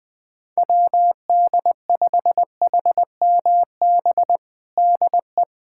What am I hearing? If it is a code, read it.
Morse code, heard as a keyed tone, about 21 wpm, 700 Hz. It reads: WD5HMB DE
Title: USA Callsign DE Callsign 20wpm